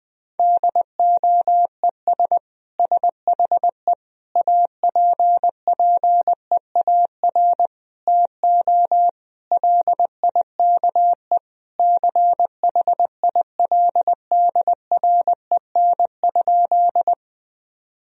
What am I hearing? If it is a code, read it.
Morse code, heard as a keyed tone, about 20 wpm, 700 Hz. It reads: DOES SHE APPEAR TO LIKE CHILDREN?
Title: Sentences from Top 1000 Words - Review Part 2 - 20wpm